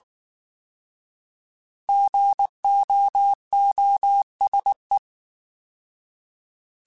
Morse code, heard as a keyed tone, about 19 wpm, 800 Hz. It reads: GOOSE